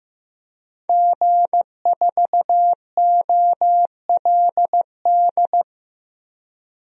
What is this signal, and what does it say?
Morse code, heard as a keyed tone, about 15 wpm, 700 Hz. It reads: G4OLD
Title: DXCC-100 Call Signs - 5 Characters - 15wpm